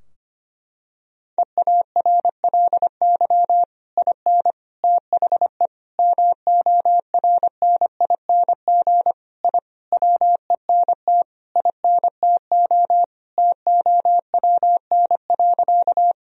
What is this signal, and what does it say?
Morse code, heard as a keyed tone, about 25 wpm, 700 Hz. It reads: EARLY IN THE MORNING I WENT INTO TOWN.
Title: Sentences from Top 1000 Words - Review Part 2 - 25wpm